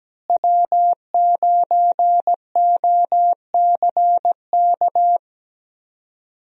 Morse code, heard as a keyed tone, about 17 wpm, 700 Hz. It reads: W9OCK
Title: USA Call Signs - 5 Characters - Repeated 2x - 17wpm